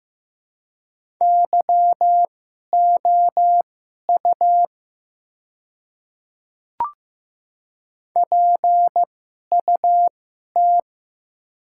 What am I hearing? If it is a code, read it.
Morse code, heard as a keyed tone, about 15 wpm, spaced as for 11 wpm, 700 Hz. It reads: YOU PUT
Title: 2x Character Spacing - Top 200 Words - 15wpm